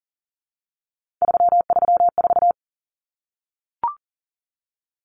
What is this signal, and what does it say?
Morse code, heard as a keyed tone, about 40 wpm, 700 Hz. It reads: 334
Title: Numbers - 3-Digit 40wpm